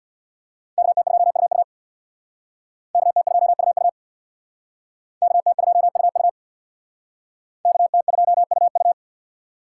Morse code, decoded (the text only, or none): KT2UU KT2UU KT2UU KT2UU